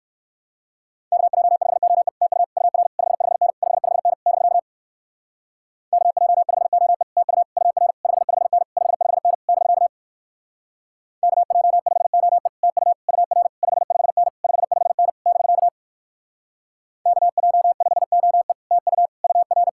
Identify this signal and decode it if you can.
Morse code, heard as a keyed tone, about 64 wpm, 700 Hz. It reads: KJ5OE TU UR 55N 55N <BK> KJ5OE TU UR 55N 55N <BK> KJ5OE TU UR 55N 55N <BK> KJ5OE TU UR